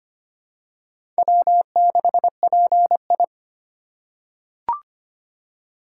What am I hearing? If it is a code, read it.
Morse code, heard as a keyed tone, about 25 wpm, 700 Hz. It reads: W6PI
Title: USA Call Signs - 3x Character Spacing - 25wpm